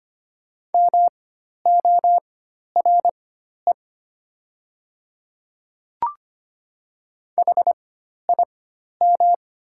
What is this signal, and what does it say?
Morse code, heard as a keyed tone, about 25 wpm, spaced as for 12 wpm, 700 Hz. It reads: MORE HIM
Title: 4x Character Spacing - Top 100 Words - 25wpm